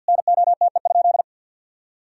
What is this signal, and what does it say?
Morse code, heard as a keyed tone, about 46 wpm, 700 Hz. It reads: NOTE?
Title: Sentences from Top 1000 Words - Review Part 2 - 50wpm